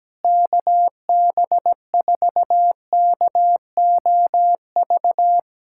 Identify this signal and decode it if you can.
Morse code, heard as a keyed tone, about 17 wpm, 700 Hz. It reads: KB4KOV
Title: USA Callsign DE Callsign - 6 Characters - 17wpm